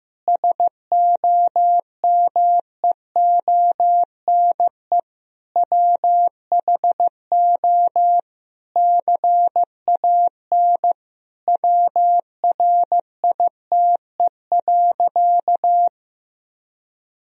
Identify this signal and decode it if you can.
Morse code, heard as a keyed tone, about 15 wpm, 700 Hz. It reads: SOMEONE WHO CAN WRITE.
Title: Sentences from Top 1000 Words - Review Part 1 - 15wpm